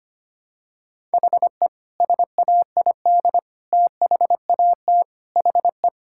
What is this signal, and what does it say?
Morse code, heard as a keyed tone, about 25 wpm, 700 Hz. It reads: HE SAID THAT HE
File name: practice / 25wpm / Sentences with 3-Word Phrases - 25wpm